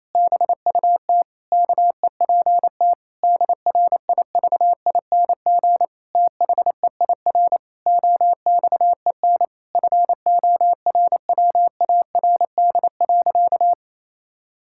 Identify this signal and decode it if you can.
Morse code, heard as a keyed tone, about 28 wpm, 700 Hz. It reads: BUT KEPT DRIVING THEIR OXEN FORWARD.
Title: Pangram Challenge v2 - 28wpm